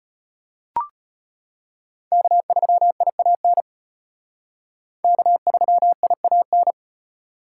Morse code, heard as a keyed tone, about 32 wpm, 700 Hz. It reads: K3IAN K3IAN